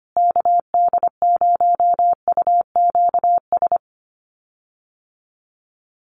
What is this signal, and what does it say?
Morse code, heard as a keyed tone, about 25 wpm, 700 Hz. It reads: KD0UQS
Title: USA Call Signs Repeated 3x 25wpm